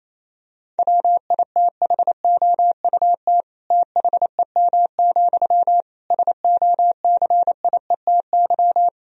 Morse code, heard as a keyed tone, about 27 wpm, 700 Hz. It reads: WITHOUT THEM, SOCIETY
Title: Sentences from Top 1000 Words - Review Part 4 - 28wpm